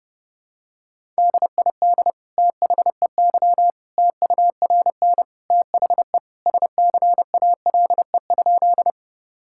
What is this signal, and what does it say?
Morse code, heard as a keyed tone, about 32 wpm, 700 Hz. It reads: DID THEY TURN THE SCALE?
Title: Sentences from Top 1000 Words - Review Part 5 - 30wpm